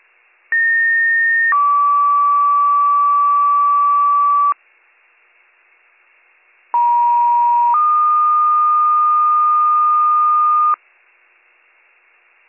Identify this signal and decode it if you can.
Two-tone page, heard as a one-second tone, then a three-second tone, about 1850 then 1150 Hz, about 950 then 1220 Hz. It is heard from 0.5 to 10.7 s.